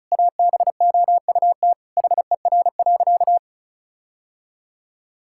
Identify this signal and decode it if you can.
Morse code, heard as a keyed tone, about 36 wpm, 700 Hz. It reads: ABOUT HER.
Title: Sentences from Top 1600 Words 35wpm